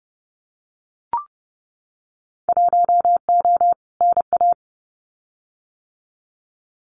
Morse code, heard as a keyed tone, about 31 wpm, 700 Hz. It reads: 1O NA
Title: Winter Field Day Exchanges 30wpm